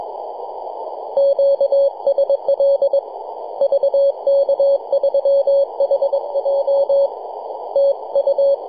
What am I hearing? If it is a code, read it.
Morse code, heard as a keyed tone, about 22 wpm, 560 Hz. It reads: QSL VK3HJ TU
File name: yj0nk-18-075mhz-cw